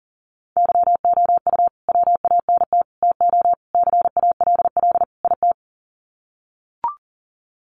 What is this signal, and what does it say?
Morse code, heard as a keyed tone, about 40 wpm, 700 Hz. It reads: YOU WANT TO CALL IT
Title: Sets of 5 Words Encore2 40wpm